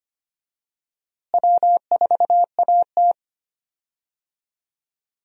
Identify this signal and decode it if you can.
Morse code, heard as a keyed tone, about 25 wpm, 700 Hz. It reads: W4AT
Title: USA Call Signs - 4 Characters - Repeated 2x - 25wpm